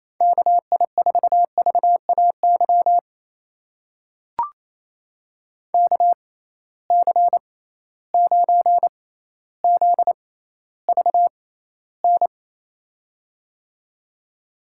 Morse code, heard as a keyed tone, about 28 wpm, spaced as for 14 wpm, 700 Hz. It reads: KI4VAY KC9ZVN
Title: USA Call Signs - 6x Character Spacing - 28wpm